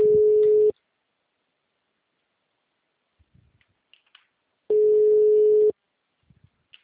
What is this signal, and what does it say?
Ringback tone (Europe, 425 Hz).
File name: out- -2011- - - .17